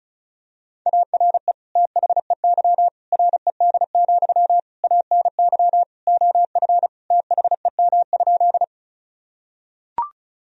Morse code, heard as a keyed tone, about 36 wpm, 700 Hz. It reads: ARE THEY RED, ANY OF THEM?
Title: Sentences from Top 1000 Words - Review Part 2 - 35wpm